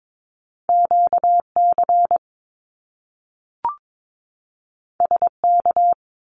Morse code, heard as a keyed tone, about 22 wpm, 700 Hz. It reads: QC SK
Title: Canadian Provinces and Territories 22wpm